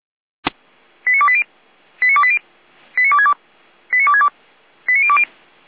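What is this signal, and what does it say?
Five-tone selective call (ZVEI), heard as quick runs of five tones, tones of about 70 ms. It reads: 89280 89280 89262 89262 89020
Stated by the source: ZVEI five-tone selective call 89320 received on an SDR.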